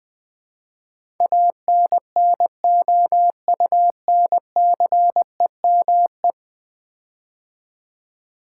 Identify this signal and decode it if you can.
Morse code, heard as a keyed tone, about 20 wpm, 700 Hz. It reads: ANNOUNCEME
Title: Build A Word - 5 Letters At A Time - 12-Letter Words - 20wpm